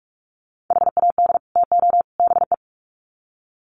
Morse code, heard as a keyed tone, about 44 wpm, 700 Hz. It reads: HAD TO BE
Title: Sets of 3 Words 45wpm